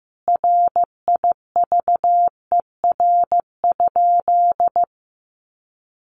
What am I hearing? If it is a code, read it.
Morse code, heard as a keyed tone, about 15 wpm, 700 Hz. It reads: RIVER?
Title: Sentences from Top 1400 Words 15wpm